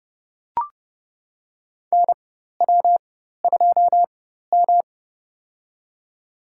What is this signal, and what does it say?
Morse code, heard as a keyed tone, about 30 wpm, spaced as for 14 wpm, 700 Hz. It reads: NW2M
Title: USA Call Signs - 4x Character Spacing - 30wpm